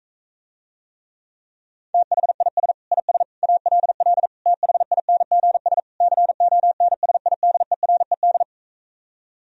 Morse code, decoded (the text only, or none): THIS IS ALL THINGS CONSIDERED